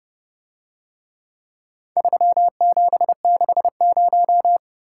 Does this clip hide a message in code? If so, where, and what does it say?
Yes: Morse code from 2.0 to 4.6 s, heard as a keyed tone, about 29 wpm, 700 Hz. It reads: 3760